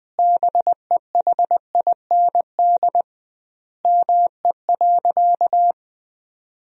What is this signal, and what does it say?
Morse code, heard as a keyed tone, about 20 wpm, 700 Hz. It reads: BEHIND ME.
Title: 2x Word-Spacing - Sentences from Top 500 Words - 20wpm